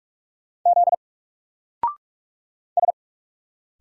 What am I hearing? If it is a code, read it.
Morse code, heard as a keyed tone, about 46 wpm, 700 Hz. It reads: Z S
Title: Single Character (Letter:Number:Punctuation:Prosign) 45wpm